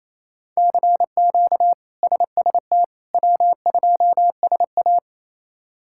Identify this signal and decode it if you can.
Morse code, heard as a keyed tone, about 28 wpm, 700 Hz. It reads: CQ SST W2SA
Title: K1USN SST QSO Practice - 28wpm